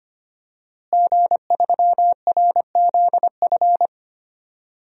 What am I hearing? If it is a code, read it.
Morse code, heard as a keyed tone, about 25 wpm, 700 Hz. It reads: G3RZF
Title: DXCC-100 Call Signs - 5 Characters - Repeated 2x - 25wpm